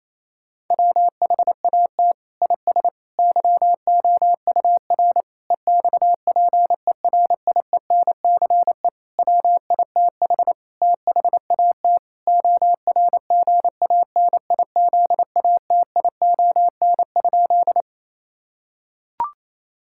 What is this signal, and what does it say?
Morse code, heard as a keyed tone, about 28 wpm, 700 Hz. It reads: WHAT IS YOUR EXPERIENCE WITH THAT ORGANIZATION?
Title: Sentences from Top 1000 Words - Review Part 5 - 28wpm